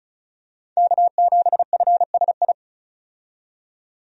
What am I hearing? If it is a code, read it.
Morse code, heard as a keyed tone, about 36 wpm, 700 Hz. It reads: K7FSI
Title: USA Call Signs - 5 Characters - 35wpm